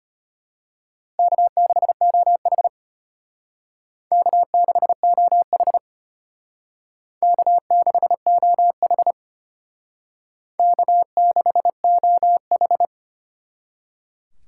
Morse code, heard as a keyed tone, about 30 wpm, 700 Hz. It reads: K6OH K6OH K6OH K6OH
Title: Speed Racing - USA 1x2 Call Signs 25wpm